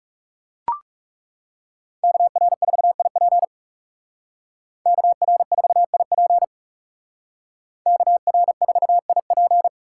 Morse code, heard as a keyed tone, about 36 wpm, 700 Hz. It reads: KR4IP KR4IP KR4IP
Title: Speed Racing - USA 2x2 Call Signs 30wpm